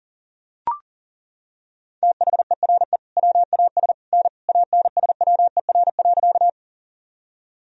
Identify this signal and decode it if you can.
Morse code, heard as a keyed tone, about 40 wpm, 700 Hz. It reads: THERE WAS N ANSWER.
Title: Sentences from Top 300 Words and Abbreviations 40wpm